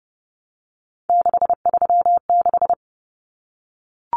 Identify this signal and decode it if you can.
Morse code, heard as a keyed tone, about 30 wpm, 700 Hz. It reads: 636
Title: Numbers - 3-Digit 30wpm